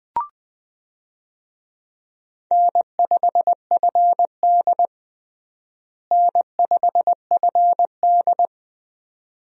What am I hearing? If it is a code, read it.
Morse code, heard as a keyed tone, about 20 wpm, 700 Hz. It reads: N5FD N5FD
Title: USA Call Signs - 4 Characters - Repeated 2x - 20wpm